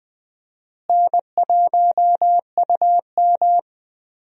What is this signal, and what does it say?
Morse code, heard as a keyed tone, about 20 wpm, 700 Hz. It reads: N1UM
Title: USA Call Signs - 4 Characters - Repeated 2x - 20wpm